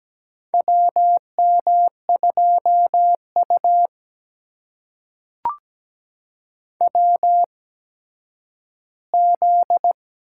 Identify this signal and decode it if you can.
Morse code, heard as a keyed tone, about 17 wpm, spaced as for 10 wpm, 700 Hz. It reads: WM2U WZ
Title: USA Call Signs - 4 Characters - 8x Character Spacing - 17wpm